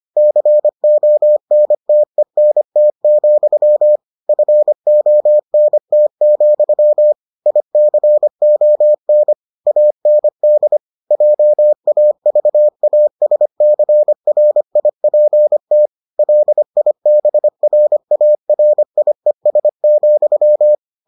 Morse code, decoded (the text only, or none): CONTENT, FONT, ICON AND JAVASCRIPT LIBRARIES,